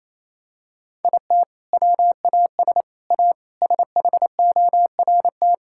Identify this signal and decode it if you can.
Morse code, heard as a keyed tone, about 28 wpm, 700 Hz. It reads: IT WAS A SHORT
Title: Sentences from Top 1000 Words - Review Part 5 - 28wpm